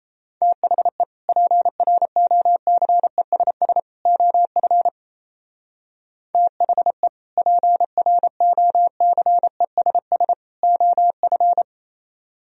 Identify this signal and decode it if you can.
Morse code, heard as a keyed tone, about 34 wpm, 700 Hz. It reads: THE PROCESS OF THE PROCESS OF
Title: Speed-Racing - Sets of 3 Words 28wpm